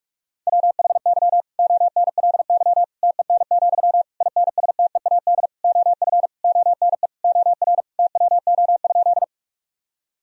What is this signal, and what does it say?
Morse code, heard as a keyed tone, about 46 wpm, 700 Hz. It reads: WHY ONLY TEN, INSTEAD OF ONE OR TWO?
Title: Sentences from Top 1000 Words - Review Part 4 - 45wpm